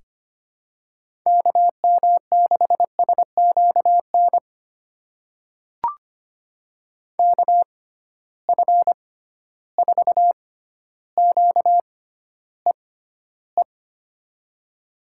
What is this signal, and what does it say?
Morse code, heard as a keyed tone, about 25 wpm, spaced as for 13 wpm, 700 Hz. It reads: KM6SQN KF4QEE
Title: USA Call Signs - 6x Character Spacing - 25wpm